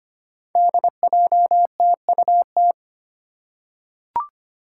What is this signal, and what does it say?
Morse code, heard as a keyed tone, about 25 wpm, 700 Hz. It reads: DJTUT
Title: Five Challenging Letter-Number Combinations 25wpm